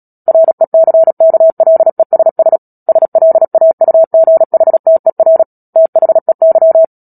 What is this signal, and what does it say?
Morse code, heard as a keyed tone, about 37 wpm, 650 Hz. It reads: RECKLESS SLAUGHTER THEY
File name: Book Two / 10-Chapter 10 - Epilogue - 36 WPM